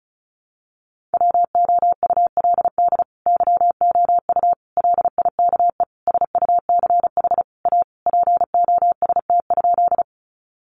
Morse code, heard as a keyed tone, about 34 wpm, 700 Hz. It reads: WOULD YOU LIKE SUCH A POST?
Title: Sentences from Top 1000 Words 35wpm